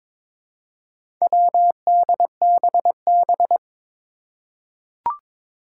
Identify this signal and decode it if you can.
Morse code, heard as a keyed tone, about 22 wpm, 700 Hz. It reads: WDBB
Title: Four Challenging Letter-Number Combinations 22wpm